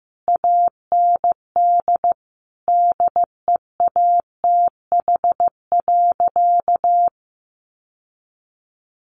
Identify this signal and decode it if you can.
Morse code, heard as a keyed tone, about 14 wpm, 700 Hz. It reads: AND DEATH.
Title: Sentences from Top 1100 Words 15wpm